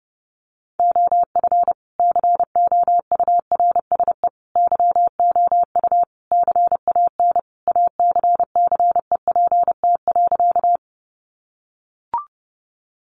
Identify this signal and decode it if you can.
Morse code, heard as a keyed tone, about 30 wpm, 700 Hz. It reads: OF COURSE YOU CAN ACCEPT.
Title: Sentences from Top 1300 Words 30wpm